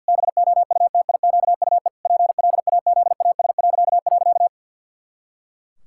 Morse code, heard as a keyed tone, about 46 wpm, 700 Hz. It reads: BOUTIQUE PLAZAS..
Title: Pangram Challenge v2 - 50wpm